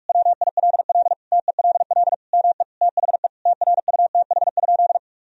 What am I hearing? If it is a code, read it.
Morse code, heard as a keyed tone, about 46 wpm, 700 Hz. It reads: WILL TELL ME THE TRUTH?
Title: Sentences from Top 1000 Words - Review Part 3 - 45wpm